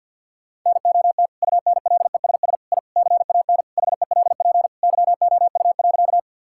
Morse code, heard as a keyed tone, about 50 wpm, 700 Hz. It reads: NOT UNLESS I CAN HELP YOU.
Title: Sentences from Top 1000 Words - Review Part 6 - 50wpm